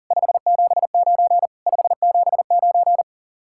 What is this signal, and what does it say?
Morse code, heard as a keyed tone, about 48 wpm, 700 Hz. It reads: 579 579
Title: Speed-Racing Common QSO Phrases Encore - 40wpm